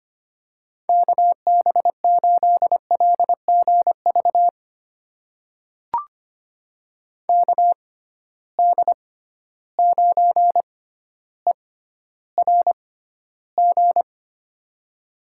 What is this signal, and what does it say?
Morse code, heard as a keyed tone, about 25 wpm, spaced as for 13 wpm, 700 Hz. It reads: KB8LGV KD9ERG
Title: USA Call Signs - 6 Characters - 6x Character Spacing - 25wpm